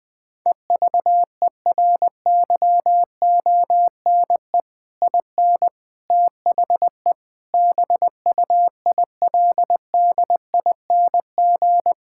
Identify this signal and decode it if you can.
Morse code, heard as a keyed tone, about 20 wpm, 700 Hz. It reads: EVERYONE IN THE BUILDING
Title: Sentences from Top 1000 Words - Review Part 6 - 20wpm